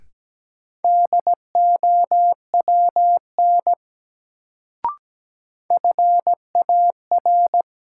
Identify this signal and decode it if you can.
Morse code, heard as a keyed tone, about 17 wpm, 700 Hz. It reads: DOWN FAR
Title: Sentences from Top 300 Words - New Word Drill - 17wpm